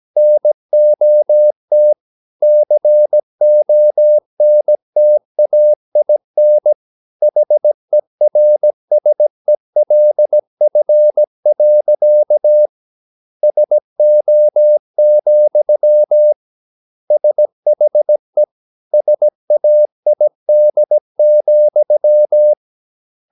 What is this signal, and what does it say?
Morse code, heard as a keyed tone, about 17 wpm, 600 Hz. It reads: NOT CONTAIN HERSELF. SO, SHE SAID,
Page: Romanen Never af Ken Follet på engelsk.